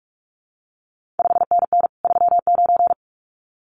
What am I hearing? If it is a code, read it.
Morse code, heard as a keyed tone, about 46 wpm, 700 Hz. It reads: 5NN 39
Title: CQ WW DX Contest Exchange Encore - 45wpm